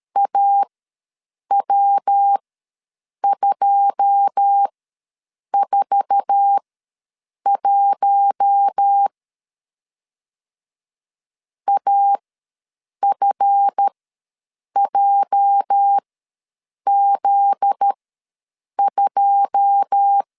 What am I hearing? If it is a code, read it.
Morse code, heard as a keyed tone, about 13 wpm, spaced as for 7 wpm, 800 Hz. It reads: AW241 AFJZ2
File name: LEZIONI / 9 286 ws 12wpm 3351